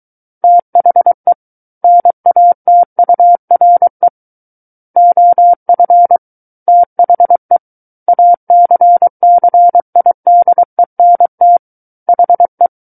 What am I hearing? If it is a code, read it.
Morse code, heard as a keyed tone, about 23 wpm, 700 Hz. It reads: THE NATURE OF THE ACCIDENT HE
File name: Book One / 14-Chapter 14 - 23 WPM